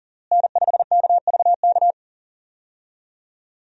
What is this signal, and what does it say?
Morse code, heard as a keyed tone, about 40 wpm, 700 Hz. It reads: N5KVK
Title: USA Call Signs - 5 Characters - Repeated 3x - 40wpm